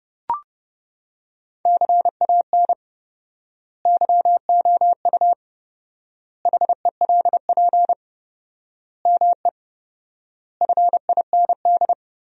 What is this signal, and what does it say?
Morse code, heard as a keyed tone, about 32 wpm, 700 Hz. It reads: CAN YOU HELP ME FIND
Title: 4x Word-Spacing - Sentences from Top 200 Words - 30wpm